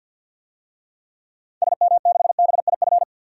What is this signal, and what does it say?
Morse code, heard as a keyed tone, about 47 wpm, 700 Hz. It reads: SM6BIF
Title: DXCC-100 Call Signs - 6 Characters - 50wpm